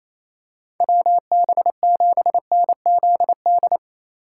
Morse code, heard as a keyed tone, about 28 wpm, 700 Hz. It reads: WB7NZD
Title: USA Call Signs - 6 Characters - 4x Character Spacing - 28wpm